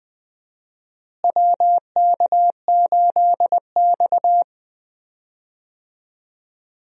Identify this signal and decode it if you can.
Morse code, heard as a keyed tone, about 21 wpm, 700 Hz. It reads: WK8X
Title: USA Call Signs - 4 Characters - Repeated 3x - 20wpm